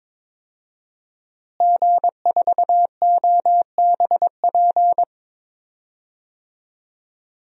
Morse code, heard as a keyed tone, about 22 wpm, 700 Hz. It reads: G4OBP
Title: DXCC-100 Call Signs - 5 Characters - Repeated 2x - 22wpm